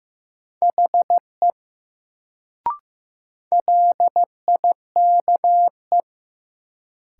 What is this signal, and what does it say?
Morse code, heard as a keyed tone, about 15 wpm, 700 Hz. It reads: HE LIKE
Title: Sentences from Top 100 Words - New Word Drill - 15wpm